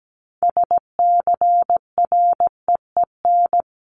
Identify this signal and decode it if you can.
Morse code, heard as a keyed tone, about 17 wpm, 700 Hz. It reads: SCREEN
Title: Top 751-1000 Words 17wpm